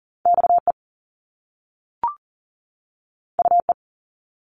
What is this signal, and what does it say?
Morse code, heard as a keyed tone, about 38 wpm, 700 Hz. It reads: XE UE
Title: Top Two Letter Combinations in Words 40wpm